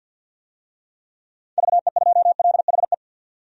Morse code, heard as a keyed tone, about 47 wpm, 700 Hz. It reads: VE2LHE
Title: DXCC-100 Call Signs - 6 Characters - 50wpm